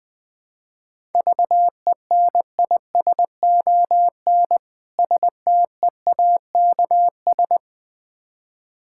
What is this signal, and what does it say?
Morse code, heard as a keyed tone, about 20 wpm, 700 Hz. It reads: VENISON STEAKS
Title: Difficult Words with Hints - 20wpm